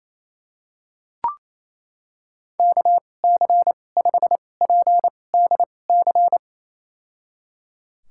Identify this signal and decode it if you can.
Morse code, heard as a keyed tone, about 28 wpm, spaced as for 20 wpm, 700 Hz. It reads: KC5PDC
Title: USA Call Signs - 6 Characters - 2x Character Spacing - 28wpm